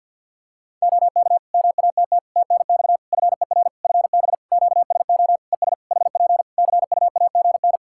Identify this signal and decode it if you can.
Morse code, heard as a keyed tone, about 51 wpm, 700 Hz. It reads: OK MATT TNX FER FB QSO ES HP CUAGN
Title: Practice QSO - Exchange 4 - Ending - 50wpm